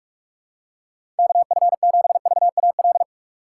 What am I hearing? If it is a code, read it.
Morse code, heard as a keyed tone, about 46 wpm, 700 Hz. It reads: KF7VAL
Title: USA Call Signs - 6 Characters - 3x Character Spacing - 45wpm